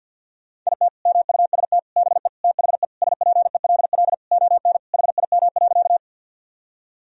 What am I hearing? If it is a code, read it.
Morse code, heard as a keyed tone, about 51 wpm, 700 Hz. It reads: IT MUST BE THE SPELL ON HIM.
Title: Sentences from Top 1000 Words - Review Part 1 - 50wpm